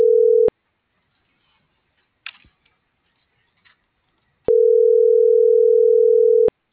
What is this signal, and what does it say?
Ringback tone (North America), 440 + 480 Hz.